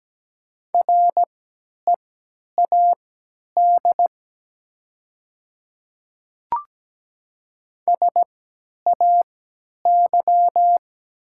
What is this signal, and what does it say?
Morse code, heard as a keyed tone, about 17 wpm, spaced as for 10 wpm, 700 Hz. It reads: READ SAY
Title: 3x Character Spacing - Top 200 Words - 17wpm